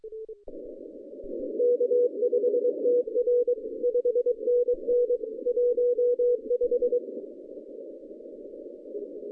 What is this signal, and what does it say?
Morse code, heard as a keyed tone, about 23 wpm, 490 Hz. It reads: K5TR 5NN 15
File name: 2015 cq ww cw / 2015 cqwww k5tr - 0-29